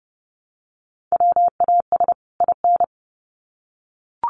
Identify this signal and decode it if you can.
Morse code, heard as a keyed tone, about 30 wpm, 700 Hz. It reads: WAS IN
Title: Sets of 2 Words 30wpm